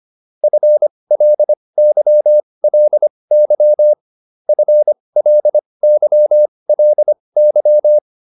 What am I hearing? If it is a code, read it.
Morse code, heard as a keyed tone, about 25 wpm, spaced as for 20 wpm, 600 Hz. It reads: FLYLY FLYLY